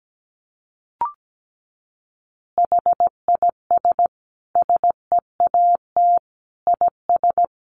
Morse code, heard as a keyed tone, about 17 wpm, 700 Hz. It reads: HIS SEAT IS
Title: Additional Easy Phrases 17wpm